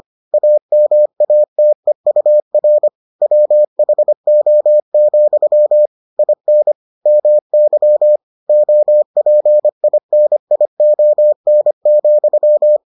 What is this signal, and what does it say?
Morse code, heard as a keyed tone, about 25 wpm, 600 Hz. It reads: AMATEUR WHO, IN MY OPINION,